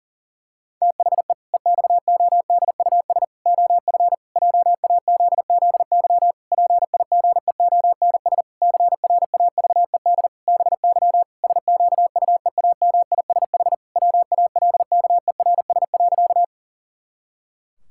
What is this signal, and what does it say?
Morse code, heard as a keyed tone, about 40 wpm, 700 Hz. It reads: THE EXODUS OF JAZZY PIGEONS CRAVED BY SQUEAMISH WALKERS.
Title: Pangram Challenge v2 - 40wpm